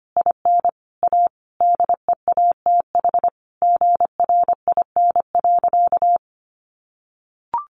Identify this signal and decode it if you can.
Morse code, heard as a keyed tone, about 25 wpm, 700 Hz. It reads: IN A DEATH GRIN.
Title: A Sound Of Thunder 25wpm